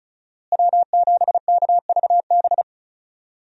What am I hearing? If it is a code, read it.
Morse code, heard as a keyed tone, about 36 wpm, 700 Hz. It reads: W7KVB
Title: USA Call Signs - 5 Characters - Repeated 3x - 35wpm